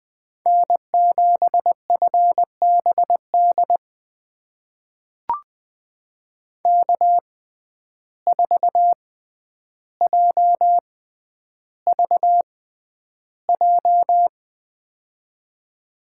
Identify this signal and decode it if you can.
Morse code, heard as a keyed tone, about 20 wpm, spaced as for 10 wpm, 700 Hz. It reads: N7FBD K4JVJ
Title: USA Call Signs - 6x Character Spacing - 20wpm